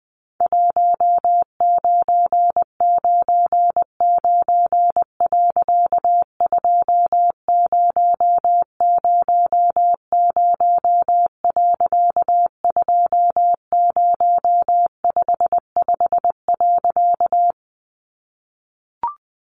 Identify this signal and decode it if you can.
Morse code, heard as a keyed tone, about 20 wpm, 700 Hz. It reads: 1999. 2000. 2055.
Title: A Sound Of Thunder 20wpm